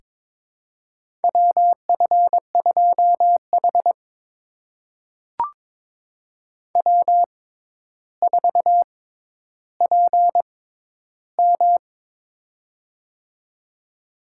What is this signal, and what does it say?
Morse code, heard as a keyed tone, about 22 wpm, spaced as for 11 wpm, 700 Hz. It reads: WF2H W4PM
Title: USA Call Signs - 4 Characters - 6x Character Spacing - 22wpm